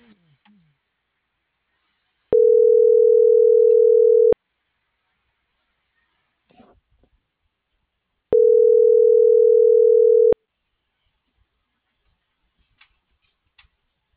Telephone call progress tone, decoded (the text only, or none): ringback tone